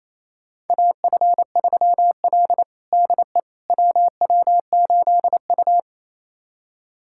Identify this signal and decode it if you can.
Morse code, heard as a keyed tone, about 28 wpm, 700 Hz. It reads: AF3L DE WW8U
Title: USA Callsign DE Callsign - 4 Characters - 28wpm